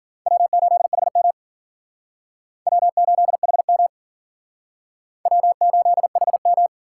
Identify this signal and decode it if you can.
Morse code, heard as a keyed tone, about 47 wpm, 700 Hz. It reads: W8HM W8HM W8HM
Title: Speed Racing - USA 1x2 Call Signs 40wpm